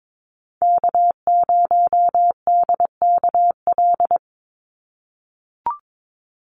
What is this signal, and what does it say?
Morse code, heard as a keyed tone, about 22 wpm, 700 Hz. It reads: K0DKL
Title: USA Call Signs 22wpm